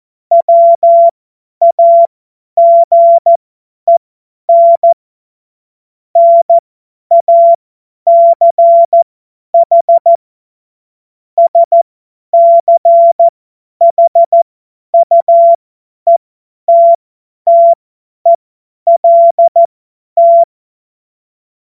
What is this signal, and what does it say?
Morse code, heard as a keyed tone, about 14 wpm, spaced as for 10 wpm, 680 Hz. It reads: WAGEN NACH SCHUETTELT
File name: Morsekurs / Witze / Witz 02. Dezember 2025 50 F70